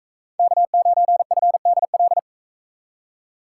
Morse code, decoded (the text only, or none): K9FDL